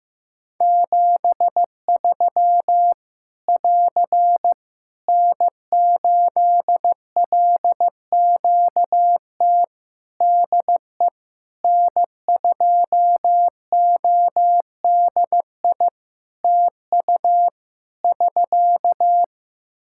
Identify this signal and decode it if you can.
Morse code, heard as a keyed tone, about 15 wpm, 700 Hz. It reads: 73 <AR> N8LQT DE N2ODI TU <SK>
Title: K1USN SST QSO Practice 15wpm